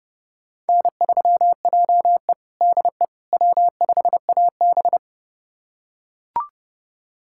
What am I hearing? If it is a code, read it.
Morse code, heard as a keyed tone, about 32 wpm, 700 Hz. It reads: N3JE DE W5AB
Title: USA Callsign DE Callsign - 4 Characters - 30wpm